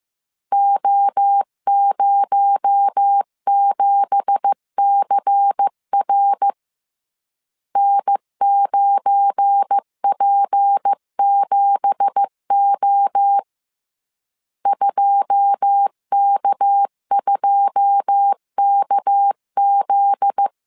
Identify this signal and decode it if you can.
Morse code, heard as a keyed tone, about 15 wpm, 800 Hz. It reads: O07CR N9P7O 2K2KZ
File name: CW / DET / 1 9 1500C S C 14wpm 3245